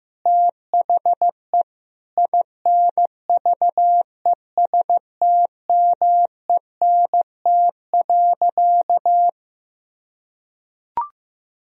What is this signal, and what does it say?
Morse code, heard as a keyed tone, about 15 wpm, 700 Hz. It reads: THE INVESTMENT.
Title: Sentences from Top 1000 Words - Review Part 4 - 15wpm